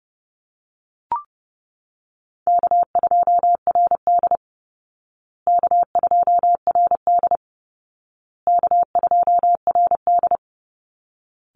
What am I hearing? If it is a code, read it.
Morse code, heard as a keyed tone, about 30 wpm, 700 Hz. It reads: K2RD K2RD K2RD
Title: Easy Call Signs Repeated 3x - 30wpm